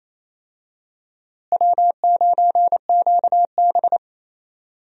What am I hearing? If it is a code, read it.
Morse code, heard as a keyed tone, about 28 wpm, 700 Hz. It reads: W9QB
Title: USA Call Signs - 4 Characters - Repeated 2x - 28wpm